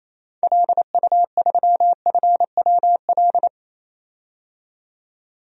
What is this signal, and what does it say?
Morse code, heard as a keyed tone, about 27 wpm, 700 Hz. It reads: LU3FWL
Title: DXCC-100 Call Signs - 6 Characters - 28wpm